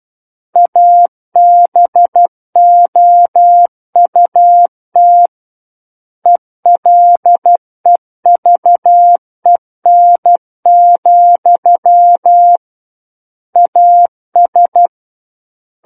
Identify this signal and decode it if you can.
Morse code, heard as a keyed tone, about 12 wpm, 700 Hz. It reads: ABOUT ELEVEN, AS
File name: Book One / 03-Chapter 3 - 12 WPM